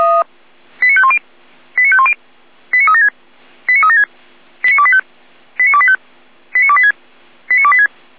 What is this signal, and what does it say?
Five-tone selective call (ZVEI), heard as quick runs of five tones, tones of about 70 ms. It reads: 89510 89510 89376 89376 89285 89285 89286 89286
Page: SDR/ZVEI